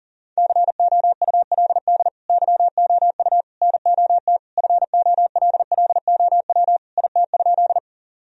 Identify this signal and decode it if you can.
Morse code, heard as a keyed tone, about 40 wpm, 700 Hz. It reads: COULD YOU NOT FOLLOW IT?
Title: Sentences from Top 1000 Words - Review Part 1 - 40wpm